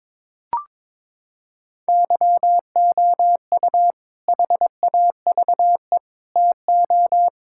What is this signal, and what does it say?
Morse code, heard as a keyed tone, about 22 wpm, 700 Hz. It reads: YOU HAVE TO
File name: practice / 22wpm / Sentences with 2-Word Phrases - 22wpm